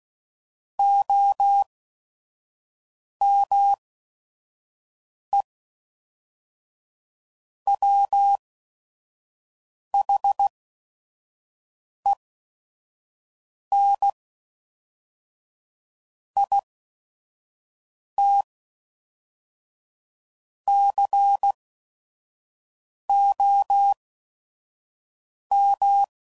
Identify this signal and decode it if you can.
Morse code, heard as a keyed tone, about 16 wpm, spaced as for 5 wpm, 800 Hz. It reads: OME WHEN IT COM
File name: CW Code Groups @5WPM / Technical Subjects / AND Gate